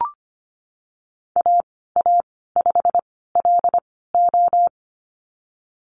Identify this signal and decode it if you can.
Morse code, heard as a keyed tone, about 25 wpm, spaced as for 16 wpm, 700 Hz. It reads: AA5LO
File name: Lesson 15g - Number 5 - Call Signs 25wpm